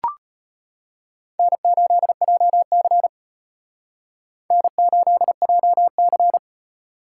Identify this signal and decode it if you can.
Morse code, heard as a keyed tone, about 32 wpm, 700 Hz. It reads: N8JC N8JC